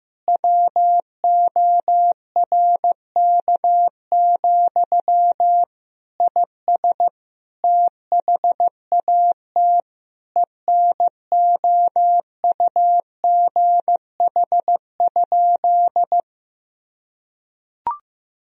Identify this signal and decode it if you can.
Morse code, heard as a keyed tone, about 15 wpm, 700 Hz. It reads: WORK, IS THAT ENOUGH?
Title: Sentences from Top 1000 Words - Review Part 2 - 15wpm